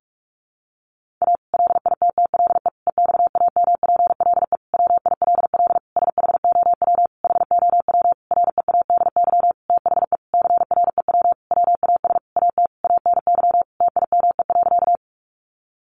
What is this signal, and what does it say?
Morse code, heard as a keyed tone, about 44 wpm, 700 Hz. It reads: A LITTLE EXAMPLE WILL SHOW HOW READY THE CREW WAS AT ANY TIME.
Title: Sentences from Top 1500 Words 45wpm